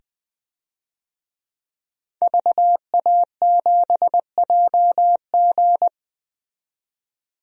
Morse code, heard as a keyed tone, about 20 wpm, 700 Hz. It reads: VA7JG
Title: DXCC-100 Call Signs - 5 Characters - 20wpm